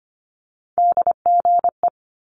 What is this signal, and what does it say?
Morse code, heard as a keyed tone, about 25 wpm, 700 Hz. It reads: DGE
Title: Multi-letter Phonemes 25wpm